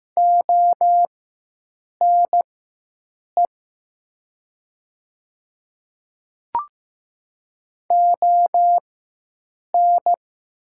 Morse code, heard as a keyed tone, about 15 wpm, spaced as for 7 wpm, 700 Hz. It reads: ONE ON
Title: 4x Character Spacing - Top 100 Words - 15wpm